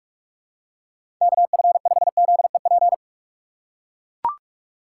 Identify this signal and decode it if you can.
Morse code, heard as a keyed tone, about 44 wpm, 700 Hz. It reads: KF5ZEP
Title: USA Call Signs - 6 Characters - 6x Character Spacing - 45wpm